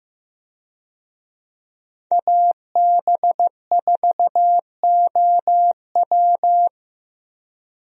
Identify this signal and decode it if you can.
Morse code, heard as a keyed tone, about 15 wpm, 700 Hz. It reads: AB4OW
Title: USA Call Signs - 5 Characters - Repeated 3x - 15wpm